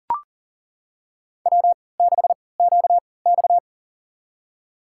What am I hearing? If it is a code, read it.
Morse code, heard as a keyed tone, about 40 wpm, spaced as for 23 wpm, 700 Hz. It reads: W6QX
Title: USA Call Signs - 4 Characters - 3x Character Spacing - 40wpm